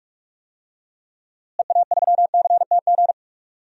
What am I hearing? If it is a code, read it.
Morse code, heard as a keyed tone, about 45 wpm, 700 Hz. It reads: EA3CTG
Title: DXCC-100 Call Signs - 6 Characters - Repeated 3x - 45wpm